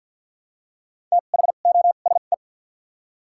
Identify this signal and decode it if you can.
Morse code, heard as a keyed tone, about 53 wpm, spaced as for 37 wpm, 700 Hz. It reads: THOSE
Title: 2x Character Spacing - Top 300 Words - 50wpm